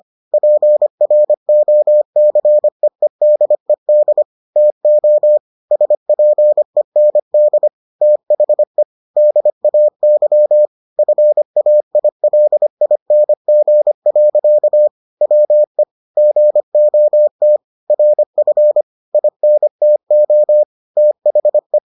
Morse code, heard as a keyed tone, about 25 wpm, 600 Hz. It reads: PROCEEDED TO SPEND THE DAY FAILING. WE GOT RF INTO THE